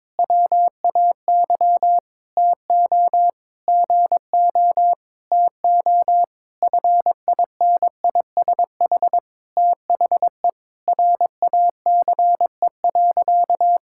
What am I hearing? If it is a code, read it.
Morse code, heard as a keyed tone, about 21 wpm, 700 Hz. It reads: WAY TO GO TO FINISH THE RACE.
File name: practice / 22wpm / Sentences with 8-Word Phrases - 22wpm